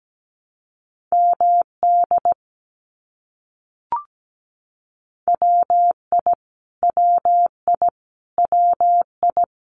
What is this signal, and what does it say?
Morse code, heard as a keyed tone, about 17 wpm, 700 Hz. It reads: MD WI WI WI
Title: US States Repeated 3x 17wpm